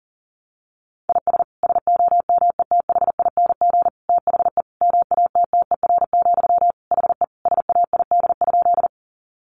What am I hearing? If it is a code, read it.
Morse code, heard as a keyed tone, about 40 wpm, 700 Hz. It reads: IS SOMETHING THE MATTER, HE SAID?
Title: Sentences from Top 700 Words 40wpm